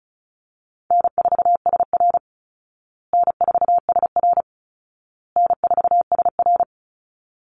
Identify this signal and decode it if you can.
Morse code, heard as a keyed tone, about 34 wpm, 700 Hz. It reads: N4SR N4SR N4SR
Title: Easy Call Signs Repeated 3x - 35wpm